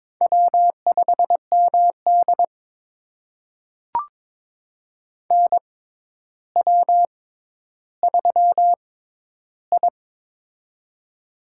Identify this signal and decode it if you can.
Morse code, heard as a keyed tone, about 22 wpm, spaced as for 11 wpm, 700 Hz. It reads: W5MD NW3I
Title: USA Call Signs - 4 Characters - 6x Character Spacing - 22wpm